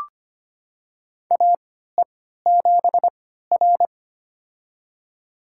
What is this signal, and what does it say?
Morse code, heard as a keyed tone, about 25 wpm, spaced as for 14 wpm, 700 Hz. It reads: AE7R
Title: USA Call Signs - 4 Characters - 3x Character Spacing - 25wpm